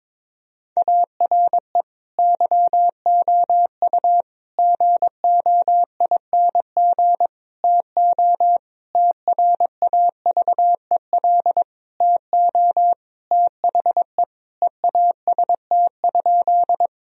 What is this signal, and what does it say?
Morse code, heard as a keyed tone, about 22 wpm, 700 Hz. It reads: ARE YOU GOING TO TRAVEL TO THE EAST?
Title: Sentences from Top 1000 Words - Review Part 3 - 22wpm